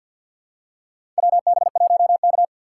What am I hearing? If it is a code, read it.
Morse code, heard as a keyed tone, about 47 wpm, 700 Hz. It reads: WB1K
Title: USA Call Signs - 4 Characters - 6x Character Spacing - 50wpm